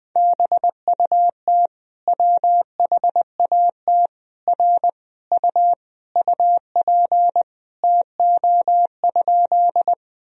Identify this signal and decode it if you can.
Morse code, heard as a keyed tone, about 20 wpm, 700 Hz. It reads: BUT WHAT R U UP TO?
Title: Sentences from Top 100 Words and Abbreviations 20wpm